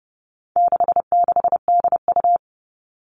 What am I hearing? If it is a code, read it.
Morse code, heard as a keyed tone, about 30 wpm, 700 Hz. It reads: 66DU
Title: All Cut Numbers - 4-Digit 30wpm